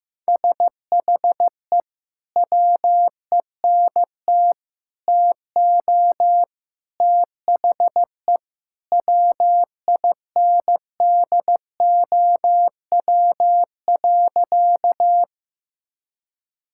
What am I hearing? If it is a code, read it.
Morse code, heard as a keyed tone, about 15 wpm, 700 Hz. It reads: SHE WENT TO THE WINDOW.
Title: Sentences from Top 1600 Words 15wpm